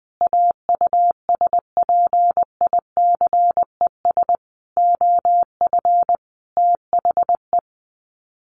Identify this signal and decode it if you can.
Morse code, heard as a keyed tone, about 20 wpm, 700 Hz. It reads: AUSPICES OF THE
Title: Sets of 5 Words Encore2 20wpm